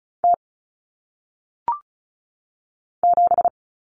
Morse code, heard as a keyed tone, about 36 wpm, 700 Hz. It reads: T 7
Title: Common Cut Numbers - 1-Digit 35wpm